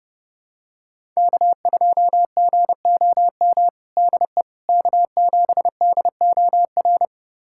Morse code, heard as a keyed tone, about 31 wpm, 700 Hz. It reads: K2GOM DE K7DOR
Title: USA Callsign DE Callsign - 5 Characters - 30wpm